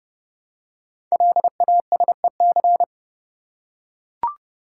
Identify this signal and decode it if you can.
Morse code, heard as a keyed tone, about 29 wpm, 700 Hz. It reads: LASEC
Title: Code Groups - 5 Letters - 30wpm